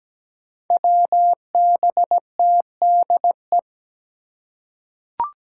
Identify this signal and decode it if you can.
Morse code, heard as a keyed tone, about 17 wpm, 700 Hz. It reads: WBTDE
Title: Five Challenging Letter-Number Combinations 17wpm